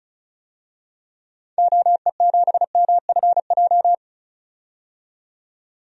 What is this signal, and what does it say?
Morse code, heard as a keyed tone, about 34 wpm, 700 Hz. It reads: OE7MFJ
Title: DXCC-100 Call Signs - 6 Characters - 35wpm